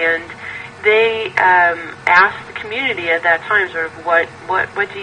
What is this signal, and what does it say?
Emergency Alert System (SAME) tones, heard as harsh buzzing bursts, with speech. It is heard from 0.3 to 3.9 s.